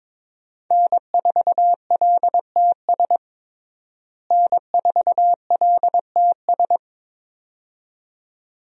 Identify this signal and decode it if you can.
Morse code, heard as a keyed tone, about 22 wpm, 700 Hz. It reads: N4LTS N4LTS
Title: USA Call Signs - 5 Characters - Repeated 2x - 22wpm